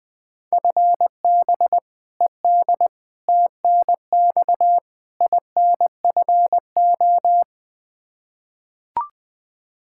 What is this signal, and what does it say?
Morse code, heard as a keyed tone, about 20 wpm, 700 Hz. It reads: FB ED TNX INFO
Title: K1USN SST QSO Practice 20wpm